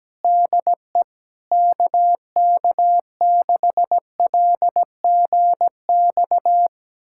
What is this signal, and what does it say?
Morse code, heard as a keyed tone, about 17 wpm, 700 Hz. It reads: DE KK6LGX
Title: USA Callsign DE Callsign - 6 Characters - 17wpm